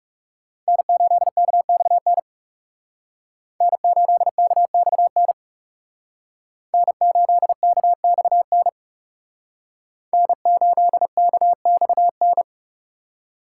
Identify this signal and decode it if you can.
Morse code, heard as a keyed tone, about 36 wpm, 700 Hz. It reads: N8KXN N8KXN N8KXN N8KXN
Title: Speed Racing - USA 1x3 Call Signs 30wpm